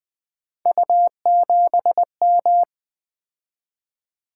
Morse code, heard as a keyed tone, about 20 wpm, 700 Hz. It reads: U7M
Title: Three Letter-Number Combinations - 20wpm